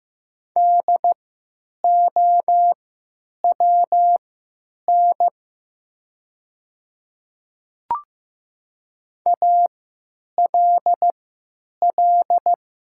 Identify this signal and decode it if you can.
Morse code, heard as a keyed tone, about 15 wpm, spaced as for 9 wpm, 700 Hz. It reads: DOWN ALL
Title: 3x Character Spacing - Top 100 Words - 15wpm